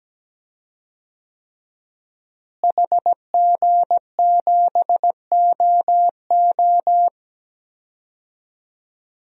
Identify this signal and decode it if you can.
Morse code, heard as a keyed tone, about 17 wpm, 700 Hz. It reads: HG7OO
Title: DXCC-100 Call Signs - 5 Characters - Repeated 3x - 17wpm